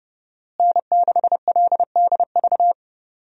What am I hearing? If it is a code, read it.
Morse code, heard as a keyed tone, about 32 wpm, 700 Hz. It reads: N6LDV
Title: USA Call Signs - 5 Characters - Repeated 2x - 30wpm